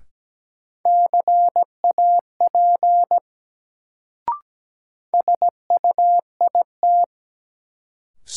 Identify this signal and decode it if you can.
Morse code, heard as a keyed tone, about 17 wpm, 700 Hz. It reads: CAP SUIT
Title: Sentences from Top 1600 Words - New Word Drill - 17wpm